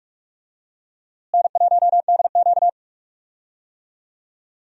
Morse code, heard as a keyed tone, about 41 wpm, 700 Hz. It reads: N1DQ
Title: USA Call Signs - 4 Characters - Repeated 2x - 45wpm